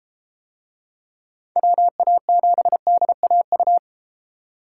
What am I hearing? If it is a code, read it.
Morse code, heard as a keyed tone, about 33 wpm, 700 Hz. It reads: WA7DAU